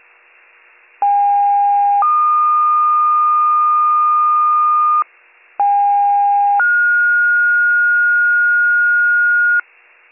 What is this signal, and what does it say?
Two-tone page, heard as a one-second tone, then a three-second tone, about 810 then 1220 Hz, about 810 then 1430 Hz, with noise. It is heard from 1.0 to 9.6 s.